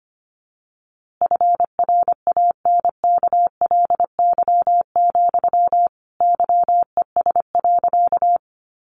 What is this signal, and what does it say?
Morse code, heard as a keyed tone, about 25 wpm, 700 Hz. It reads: FRANKLY, YES.
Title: A Sound Of Thunder 25wpm